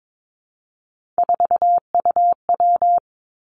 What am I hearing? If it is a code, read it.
Morse code, heard as a keyed tone, about 22 wpm, 700 Hz. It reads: 4UW
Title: Three Challenging Letter-Number Combinations 22wpm